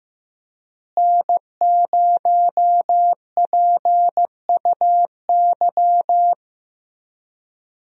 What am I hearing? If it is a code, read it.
Morse code, heard as a keyed tone, about 15 wpm, 700 Hz. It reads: N0PUY
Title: USA Call Signs - 5 Characters - Repeated 2x - 15wpm